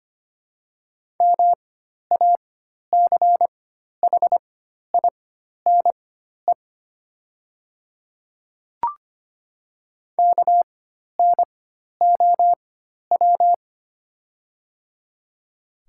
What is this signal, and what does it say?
Morse code, heard as a keyed tone, about 25 wpm, spaced as for 12 wpm, 700 Hz. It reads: MACHINE KNOW
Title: 4x Character Spacing - Top 500 Words - 25wpm